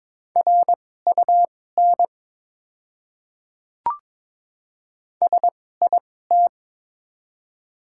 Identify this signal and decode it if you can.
Morse code, heard as a keyed tone, about 22 wpm, spaced as for 16 wpm, 690 Hz. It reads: RUN SIT
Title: 2x Character Spacing - Top 500 Words - 22wpm